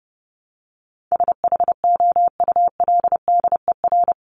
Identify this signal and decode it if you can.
Morse code, heard as a keyed tone, about 30 wpm, 700 Hz. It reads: SHOULDER
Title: Top 1501-1750 Words 30wpm